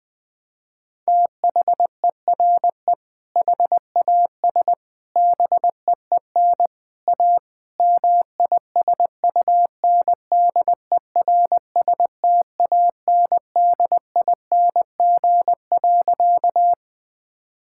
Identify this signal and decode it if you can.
Morse code, heard as a keyed tone, about 20 wpm, 700 Hz. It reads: THERE HAS BEEN A MISUNDERSTANDING.
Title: Sentences with 4-Word Phrases - 20wpm